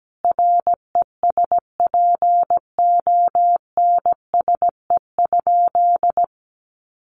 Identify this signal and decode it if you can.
Morse code, heard as a keyed tone, about 17 wpm, 700 Hz. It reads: RESPONSE?
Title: Sentences from Top 1400 Words 17wpm